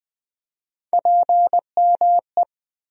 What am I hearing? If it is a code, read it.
Morse code, heard as a keyed tone, about 20 wpm, 700 Hz. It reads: PME
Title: USA Call Sign Postfixes - 3 Letters 20wpm